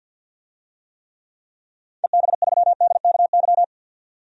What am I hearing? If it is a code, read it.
Morse code, heard as a keyed tone, about 46 wpm, 700 Hz. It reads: EB3DKY
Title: DXCC-100 Call Signs - 6 Characters - 50wpm